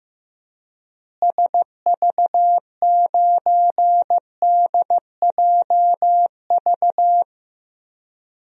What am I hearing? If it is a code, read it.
Morse code, heard as a keyed tone, about 15 wpm, 700 Hz. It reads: SV9DJV
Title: DXCC-100 Call Signs - 6 Characters - Repeated 2x - 15wpm